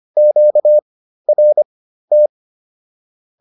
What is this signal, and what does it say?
Morse code, heard as a keyed tone, about 25 wpm, spaced as for 13 wpm, 600 Hz. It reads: QRT